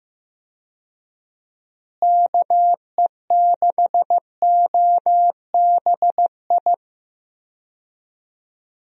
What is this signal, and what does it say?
Morse code, heard as a keyed tone, about 15 wpm, 700 Hz. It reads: KE6OBI
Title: USA Call Signs - 6 Characters - Repeated 3x - 15wpm